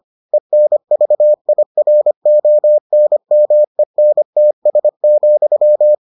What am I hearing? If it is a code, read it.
Morse code, heard as a keyed tone, about 25 wpm, 600 Hz. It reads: ENVIRONMENTS,